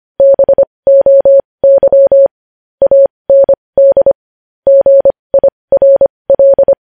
Morse code, heard as a keyed tone, about 25 wpm, spaced as for 20 wpm, 550 Hz. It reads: BOY AND GIRL
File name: podcast / News headlines-1x-25x20